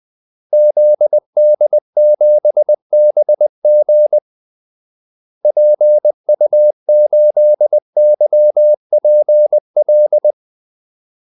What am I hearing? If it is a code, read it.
Morse code, heard as a keyed tone, about 20 wpm, 600 Hz. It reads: ZD7BG PU8YPL
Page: Kaldesignaler 20 wpm